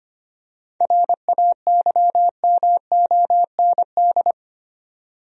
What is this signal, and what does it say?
Morse code, heard as a keyed tone, about 25 wpm, 700 Hz. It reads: RAYMOND
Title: USA Common Names - 25wpm